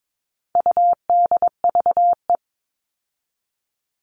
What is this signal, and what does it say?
Morse code, heard as a keyed tone, about 22 wpm, 700 Hz. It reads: UDVE
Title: All Cut Numbers - 4-Digit 22wpm